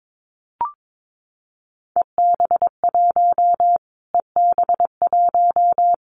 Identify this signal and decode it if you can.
Morse code, heard as a keyed tone, about 22 wpm, 700 Hz. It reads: EB1 EB1
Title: Callsign Prefixes for Top 25 DXCC - Repeated 3x - 22wpm